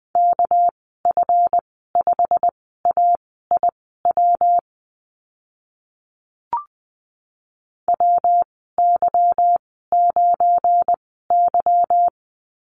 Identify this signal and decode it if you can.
Morse code, heard as a keyed tone, about 20 wpm, spaced as for 14 wpm, 700 Hz. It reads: KF5AIW WY9Y
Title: 213 Lesson 34g - Number 8 - Call Signs 20wpm